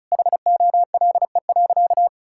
Morse code, heard as a keyed tone, about 36 wpm, 700 Hz. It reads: HOLE.
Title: Sentences from Top 1600 Words 35wpm